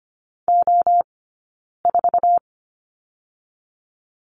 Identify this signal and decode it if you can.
Morse code, heard as a keyed tone, about 25 wpm, 700 Hz. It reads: O 4
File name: course / 25:10wpm / 092 Lesson 16d - Letter c - Two Characters 25wpm